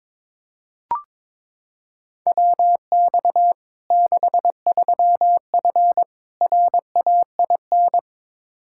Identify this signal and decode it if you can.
Morse code, heard as a keyed tone, about 22 wpm, 700 Hz. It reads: WX 63F RAIN
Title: Practice QSO - Exchange 2 - Info - 22wpm